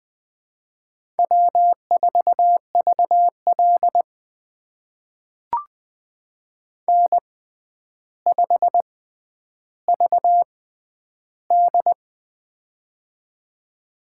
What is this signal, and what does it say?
Morse code, heard as a keyed tone, about 20 wpm, spaced as for 10 wpm, 700 Hz. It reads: W4VL N5VD
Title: USA Call Signs - 4 Characters - 6x Character Spacing - 20wpm